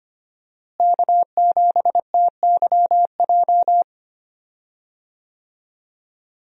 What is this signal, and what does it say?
Morse code, heard as a keyed tone, about 25 wpm, 700 Hz. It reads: K7TYJ
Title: USA Call Signs - 5 Characters - Repeated 2x - 25wpm